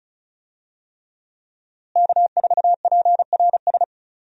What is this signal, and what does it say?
Morse code, heard as a keyed tone, about 36 wpm, 700 Hz. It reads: K4PRS
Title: USA Call Signs - 5 Characters - Repeated 3x - 35wpm